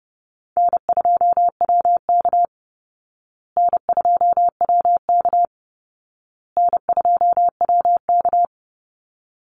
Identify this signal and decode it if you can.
Morse code, heard as a keyed tone, about 30 wpm, 700 Hz. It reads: N2WK N2WK N2WK
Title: Easy Call Signs Repeated 3x - 30wpm